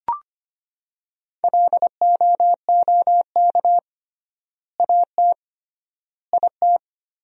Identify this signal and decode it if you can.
Morse code, heard as a keyed tone, about 25 wpm, 700 Hz. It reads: LOOK AT IT
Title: 3x Word-Spacing - Sentences from Top 200 Words - 25wpm